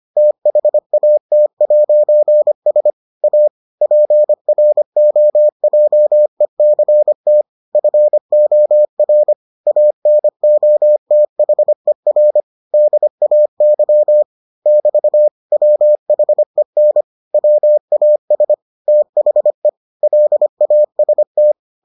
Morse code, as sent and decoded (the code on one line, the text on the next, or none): - .... .- - .----. ... / .- / .--. .-. --- .--- . -.-. - / ..-. --- .-. / .- -. --- - .... .-. / -.. .- -.-- / -...- / .-- .... -. / .-- .- ... / - .... / .-.. .- ... -
THAT'S A PROJECT FOR ANOTHER DAY <BT> WHEN WAS THE LAST